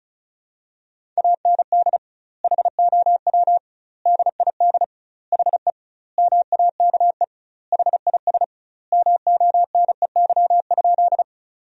Morse code, heard as a keyed tone, about 36 wpm, 700 Hz. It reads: AND HOW DID HE MAKE HIS MONEY?
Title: 2x Word-Spacing - Sentences from Top 400 Words - 35wpm